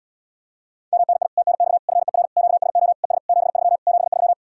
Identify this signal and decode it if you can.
Morse code, heard as a keyed tone, about 77 wpm, 700 Hz. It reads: MNI TNX FER QSO ES 73 73